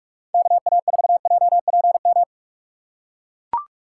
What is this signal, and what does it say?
Morse code, heard as a keyed tone, about 40 wpm, 700 Hz. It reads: KA4JPM